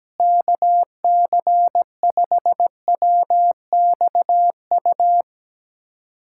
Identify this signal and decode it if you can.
Morse code, heard as a keyed tone, about 17 wpm, 700 Hz. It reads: KC5WXU
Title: USA Callsign DE Callsign - 6 Characters - 17wpm